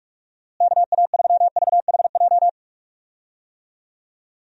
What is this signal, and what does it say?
Morse code, heard as a keyed tone, about 46 wpm, 700 Hz. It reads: KA3VHJ
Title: USA Call Signs - 6 Characters - Repeated 2x - 45wpm